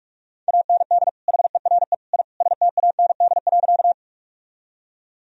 Morse code, heard as a keyed tone, about 46 wpm, 700 Hz. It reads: AND HERE I STAND.
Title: Sentences from Top 1000 Words - Review Part 1 - 45wpm